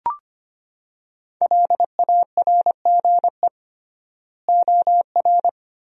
Morse code, heard as a keyed tone, about 25 wpm, 700 Hz. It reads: LARGE OR
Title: 3x Word-Spacing - Sentences from Top 200 Words - 25wpm